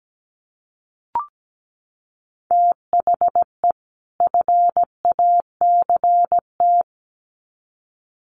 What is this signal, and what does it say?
Morse code, heard as a keyed tone, about 17 wpm, 700 Hz. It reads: THE FACT
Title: Sets of 2 Words 17wpm